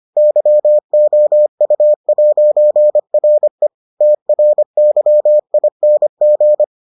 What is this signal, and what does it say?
Morse code, heard as a keyed tone, about 25 wpm, 600 Hz. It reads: YOU'RE TRYING